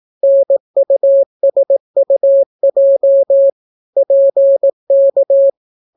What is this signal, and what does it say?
Morse code, heard as a keyed tone, about 18 wpm, 550 Hz. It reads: NUSUJ PK